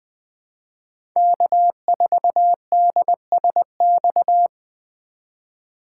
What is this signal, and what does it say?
Morse code, heard as a keyed tone, about 20 wpm, 700 Hz. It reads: K4DSX
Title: 2023 POTA Activator Call Signs - 20wpm